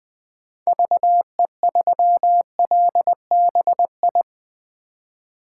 Morse code, heard as a keyed tone, about 20 wpm, 700 Hz. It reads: VE3LBI
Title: 2023 POTA Hunter Call Signs - 20wpm